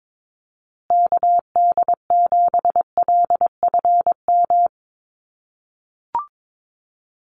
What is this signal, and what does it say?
Morse code, heard as a keyed tone, about 22 wpm, 700 Hz. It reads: KD7LFM
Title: USA Call Signs Repeated 2x 22wpm